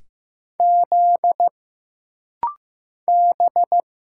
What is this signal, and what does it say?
Morse code, heard as a keyed tone, about 15 wpm, 700 Hz. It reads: Z B
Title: Single Letters - Rapid-Fire 15wpm